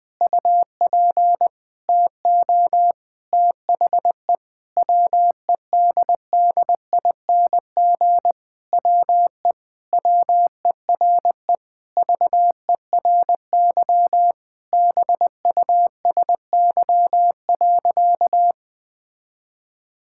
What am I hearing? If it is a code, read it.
Morse code, heard as a keyed tone, about 20 wpm, 700 Hz. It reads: UP TO THE WEDDING WE WERE VERY BUSY.
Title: Sentences with 7-Word Phrases - 20wpm